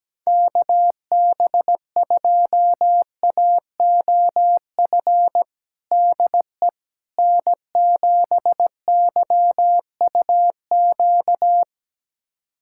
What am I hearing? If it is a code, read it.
Morse code, heard as a keyed tone, about 17 wpm, 700 Hz. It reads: KB2AOF DE N7YUQ
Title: USA Callsign DE Callsign 17wpm